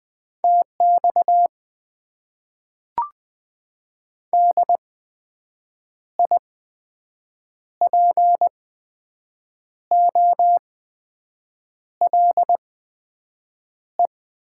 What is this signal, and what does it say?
Morse code, heard as a keyed tone, about 20 wpm, spaced as for 6 wpm, 700 Hz. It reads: TX DIPOLE
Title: Top 90 QSO Elements - 8x Character Spacing - 20wpm